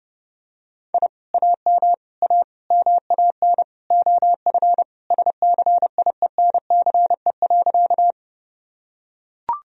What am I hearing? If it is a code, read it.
Morse code, heard as a keyed tone, about 32 wpm, 700 Hz. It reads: I AM A MAN OF SCIENCE.
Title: Sentences from Top 1000 Words - Review Part 2 - 30wpm